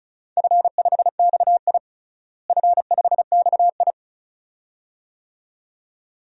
Morse code, heard as a keyed tone, about 34 wpm, 700 Hz. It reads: F5XI F5XI
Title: DXCC-100 Call Signs - 4 Characters - Repeated 2x - 35wpm